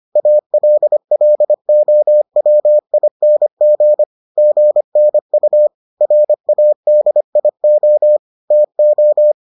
Morse code, heard as a keyed tone, about 25 wpm, 600 Hz. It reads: ALLOWING GNU RADIO TO